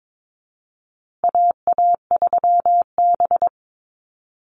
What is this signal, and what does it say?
Morse code, heard as a keyed tone, about 22 wpm, 700 Hz. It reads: AA3B
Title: Call Signs Repeated 2x - 22wpm